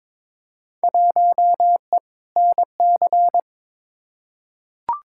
Morse code, heard as a keyed tone, about 22 wpm, 700 Hz. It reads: 1E NC
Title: Field Day Exchanges - 22wpm